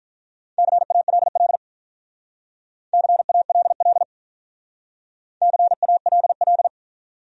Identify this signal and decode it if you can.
Morse code, heard as a keyed tone, about 43 wpm, 700 Hz. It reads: CALL CALL CALL
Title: Speed-Racing Top 200 Words 35wpm